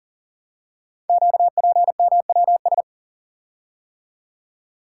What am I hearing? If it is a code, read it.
Morse code, heard as a keyed tone, about 42 wpm, 700 Hz. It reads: QPMWS
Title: Code Groups - 5 Letters - 40wpm